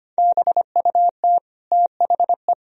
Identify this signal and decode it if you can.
Morse code, heard as a keyed tone, about 25 wpm, 700 Hz. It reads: BUT THE
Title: Sentences from Top 1000 Words - Review Part 3 - 25wpm